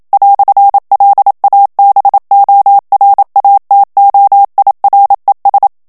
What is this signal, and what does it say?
Morse code, heard as a keyed tone, about 27 wpm, 800 Hz. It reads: "LABORATOIRES
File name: CW / FICHIERS-MP3 / FRANCAIS TEXTE CLAIR 1 VITESSE 26 MOTS MINUTES